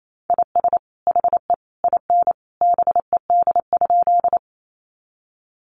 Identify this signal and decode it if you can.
Morse code, heard as a keyed tone, about 28 wpm, 700 Hz. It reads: IS HE IN BED?
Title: Sentences from Top 500 Words 28wpm